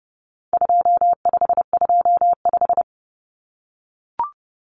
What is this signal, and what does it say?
Morse code, heard as a keyed tone, about 30 wpm, 700 Hz. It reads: 2525
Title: Common Cut Numbers - 4-Digit 30wpm